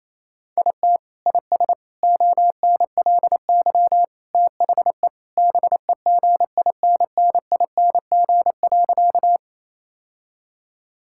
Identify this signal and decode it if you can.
Morse code, heard as a keyed tone, about 28 wpm, 700 Hz. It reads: IT IS ONLY THE BEGINNING.
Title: Sentences from Top 1000 Words - Review Part 5 - 28wpm